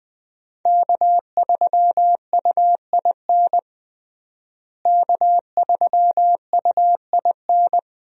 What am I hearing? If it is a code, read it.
Morse code, heard as a keyed tone, about 21 wpm, 700 Hz. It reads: K3UIN K3UIN
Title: USA Call Signs - 5 Characters - Repeated 3x - 20wpm